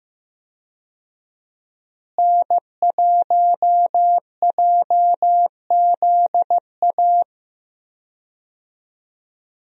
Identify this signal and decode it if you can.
Morse code, heard as a keyed tone, about 15 wpm, 700 Hz. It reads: N1JZA
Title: USA Call Signs - 5 Characters - Repeated 2x - 15wpm